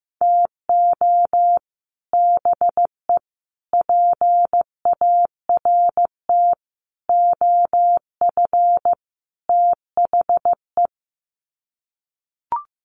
Morse code, heard as a keyed tone, about 15 wpm, 700 Hz. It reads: TO BE PART OF THE
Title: Sets of 5 Words 15wpm